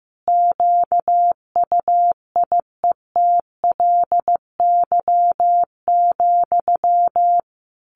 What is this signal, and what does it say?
Morse code, heard as a keyed tone, about 14 wpm, 700 Hz. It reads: QUIETLY,
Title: A Sound Of Thunder 15wpm